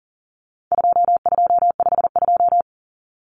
Morse code, heard as a keyed tone, about 37 wpm, 700 Hz. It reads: 2252
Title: Common Cut Numbers - 4-Digit 40wpm